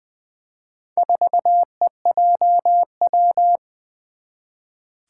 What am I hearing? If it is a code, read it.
Morse code, heard as a keyed tone, about 20 wpm, 700 Hz. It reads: 4EJW
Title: Four Challenging Letter-Number Combinations 20wpm